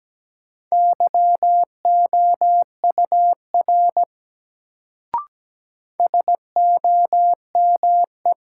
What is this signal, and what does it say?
Morse code, heard as a keyed tone, about 17 wpm, 700 Hz. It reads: YOUR SOME
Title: Sets of 2 Words - New Word Drill - 17wpm